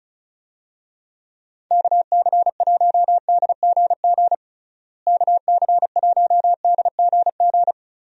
Morse code, heard as a keyed tone, about 34 wpm, 700 Hz. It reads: KC1DGG KC1DGG
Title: USA Call Signs - 6 Characters - Repeated 2x - 35wpm